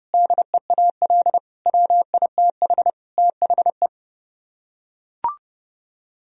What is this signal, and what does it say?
Morse code, heard as a keyed tone, about 28 wpm, 700 Hz. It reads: DEAL WITH THE